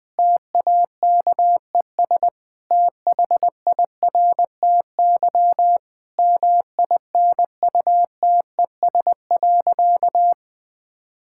Morse code, heard as a keyed tone, about 20 wpm, 700 Hz. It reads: TAKES THIRTY MINUTES.
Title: Difficult Sentences with Hints - 20wpm